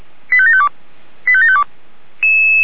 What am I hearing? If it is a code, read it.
Five-tone selective call (ZVEI), heard as quick runs of five tones, tones of about 70 ms. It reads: 85752 85752